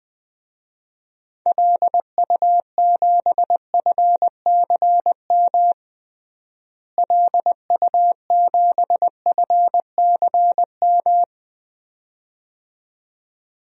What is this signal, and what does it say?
Morse code, heard as a keyed tone, about 20 wpm, 700 Hz. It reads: LU7FCM LU7FCM
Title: DXCC-100 Call Signs - 6 Characters - Repeated 2x - 20wpm